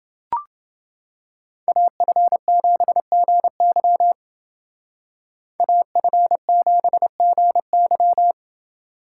Morse code, heard as a keyed tone, about 28 wpm, 700 Hz. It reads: AF7GY AF7GY